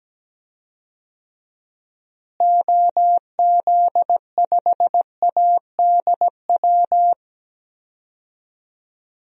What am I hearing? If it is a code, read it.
Morse code, heard as a keyed tone, about 17 wpm, 700 Hz. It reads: OZ5ADW
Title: DXCC-100 Call Signs - 6 Characters - Repeated 3x - 17wpm